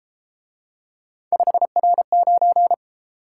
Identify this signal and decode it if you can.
Morse code, heard as a keyed tone, about 30 wpm, 700 Hz. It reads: 5R9